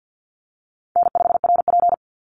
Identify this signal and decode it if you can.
Morse code, heard as a keyed tone, about 50 wpm, 700 Hz. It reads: N5RP
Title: Easy Call Signs Repeated 2x - 50wpm